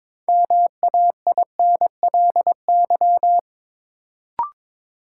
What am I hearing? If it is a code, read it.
Morse code, heard as a keyed tone, about 22 wpm, 700 Hz. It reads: MAINLY
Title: Sentences from Top 1400 Words - New Word Drill - 22wpm